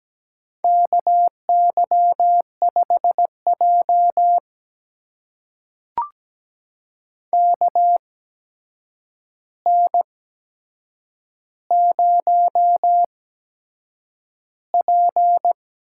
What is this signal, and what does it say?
Morse code, heard as a keyed tone, about 17 wpm, spaced as for 7 wpm, 700 Hz. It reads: KY5J KN0P
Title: USA Call Signs - 4 Characters - 8x Character Spacing - 17wpm